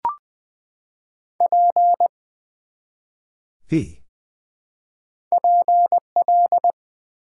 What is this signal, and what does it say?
Morse code, heard as a keyed tone, about 20 wpm, 700 Hz. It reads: P PL
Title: Build A Word - 1 Letter At A Time - 4-Letter Words - 20wpm